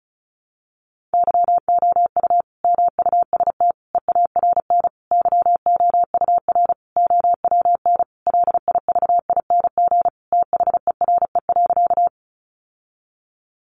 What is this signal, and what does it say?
Morse code, heard as a keyed tone, about 34 wpm, 700 Hz. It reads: YOU MUST EARN YOUR OWN LIVING THERE.
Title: Sentences from Top 1500 Words 35wpm